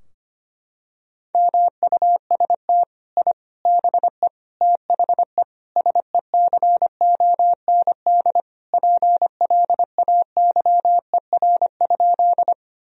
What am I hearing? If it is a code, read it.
Morse code, heard as a keyed tone, about 25 wpm, 700 Hz. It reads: MUST I BE THE SECOND PLAYER?
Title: Sentences from Top 1000 Words - Review Part 4 - 25wpm